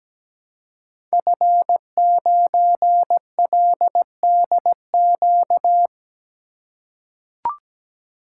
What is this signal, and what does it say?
Morse code, heard as a keyed tone, about 17 wpm, 700 Hz. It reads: F9LDQ
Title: Code Groups - 5 Letter:Numbers - 17wpm